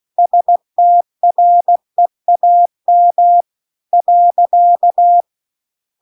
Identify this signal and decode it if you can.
Morse code, heard as a keyed tone, about 16 wpm, 700 Hz. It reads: STREAM .
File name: CW-The-Big-Twoheaded-River-2-16WPM